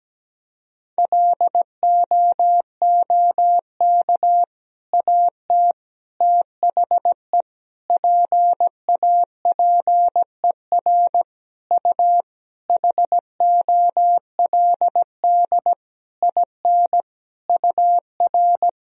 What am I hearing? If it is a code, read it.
Morse code, heard as a keyed tone, about 17 wpm, 700 Hz. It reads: LOOK AT THE PAPER U HOLD IN UR
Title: Sentences from Top 400 Words and Abbreviations 17wpm